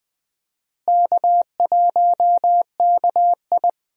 Morse code, heard as a keyed tone, about 20 wpm, 700 Hz. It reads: K1KI
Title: USA Call Signs - 4 Characters - Repeated 2x - 20wpm